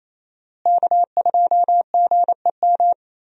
Morse code, heard as a keyed tone, about 28 wpm, 700 Hz. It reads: K2GEM
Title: USA Call Signs - 5 Characters - Repeated 2x - 28wpm